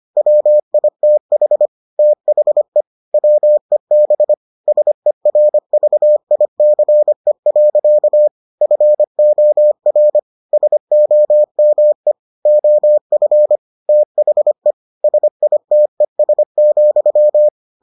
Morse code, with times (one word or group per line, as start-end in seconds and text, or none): 0.2-1.7 WITH
2.0-2.8 THE
3.1-4.3 WEB
4.7-8.3 SERVICE.
8.6-10.2 FOR
10.5-12.1 SOME
12.5-13.6 OF
13.9-14.7 THE
15.0-17.5 SITES,